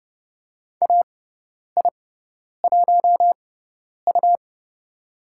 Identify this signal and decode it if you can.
Morse code, heard as a keyed tone, about 30 wpm, spaced as for 10 wpm, 690 Hz. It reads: AI1U
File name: Lesson 17g - Letter u - Call Signs 30wpm